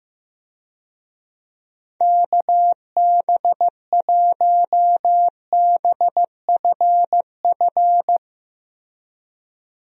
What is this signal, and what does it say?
Morse code, heard as a keyed tone, about 15 wpm, 700 Hz. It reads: KB1BFF
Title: USA Call Signs - 6 Characters - Repeated 3x - 15wpm